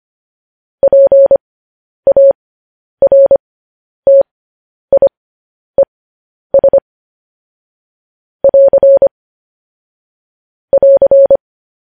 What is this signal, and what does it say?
Morse code, heard as a keyed tone, about 25 wpm, spaced as for 10 wpm, 550 Hz. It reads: PARTIES <AR> <AR>
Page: news-headlines-in-morse-code-at-10-wpm